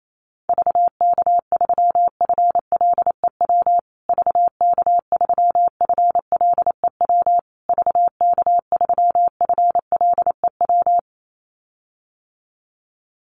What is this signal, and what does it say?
Morse code, heard as a keyed tone, about 28 wpm, 700 Hz. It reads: VK3FLEW VK3FLEW VK3FLEW
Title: USA Call Signs Repeated 3x 28wpm